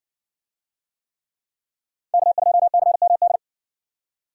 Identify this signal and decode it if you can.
Morse code, heard as a keyed tone, about 56 wpm, 700 Hz. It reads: K2CMD